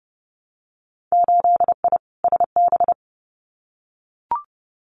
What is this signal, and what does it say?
Morse code, heard as a keyed tone, about 32 wpm, 700 Hz. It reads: 8I SB
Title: Winter Field Day Exchanges Encore 30wpm